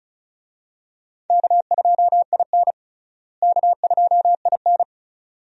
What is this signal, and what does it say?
Morse code, heard as a keyed tone, about 36 wpm, 700 Hz. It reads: K2IN K2IN
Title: USA Call Signs - 4 Characters - Repeated 2x - 35wpm